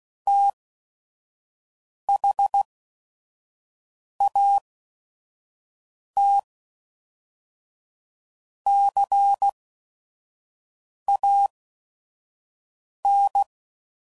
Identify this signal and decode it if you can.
Morse code, heard as a keyed tone, about 16 wpm, spaced as for 5 wpm, 800 Hz. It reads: THAT CAN
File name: CW Code Groups @5WPM / Technical Subjects / OR Gate